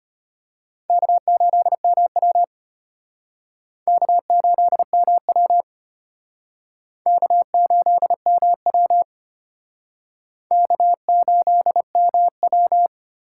Morse code, heard as a keyed tone, about 32 wpm, 700 Hz. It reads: K8MW K8MW K8MW K8MW